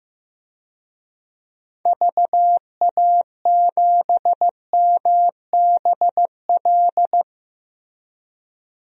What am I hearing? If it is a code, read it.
Morse code, heard as a keyed tone, about 15 wpm, 700 Hz. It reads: VA7MBL
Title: DXCC-100 Call Signs - 6 Characters - Repeated 2x - 15wpm